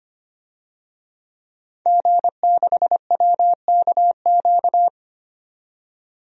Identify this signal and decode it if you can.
Morse code, heard as a keyed tone, about 25 wpm, 700 Hz. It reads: G6WKQ
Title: DXCC-100 Call Signs - 5 Characters - 25wpm